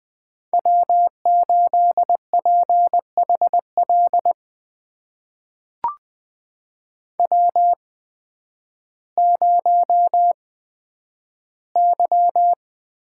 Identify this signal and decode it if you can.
Morse code, heard as a keyed tone, about 20 wpm, spaced as for 11 wpm, 700 Hz. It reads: W8PHL W0Y
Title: USA Call Signs - 8x Character Spacing - 20wpm